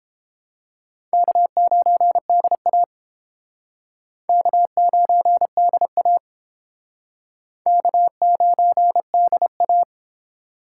Morse code, heard as a keyed tone, about 30 wpm, 700 Hz. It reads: K9DA K9DA K9DA